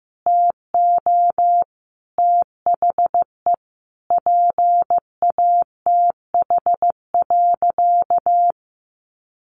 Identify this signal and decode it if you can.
Morse code, heard as a keyed tone, about 15 wpm, 700 Hz. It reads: TO THE PATH.
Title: A Sound Of Thunder 15wpm